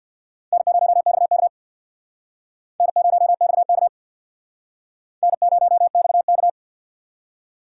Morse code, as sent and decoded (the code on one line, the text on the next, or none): -. ----- -..- -.- / -. ----- -..- -.- / -. ----- -..- -.-
N0XK N0XK N0XK